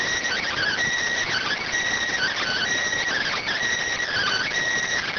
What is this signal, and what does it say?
Emergency Alert System (SAME) tones (header bursts), heard as harsh buzzing bursts. It is heard from 0.2 to 4.5 s.